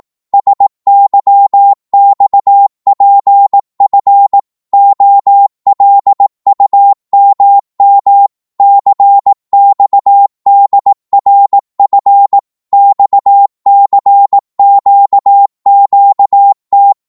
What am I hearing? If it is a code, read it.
Morse code, heard as a keyed tone, about 18 wpm, 800 Hz. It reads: SYXPF OLUMM CXDRF XCQQT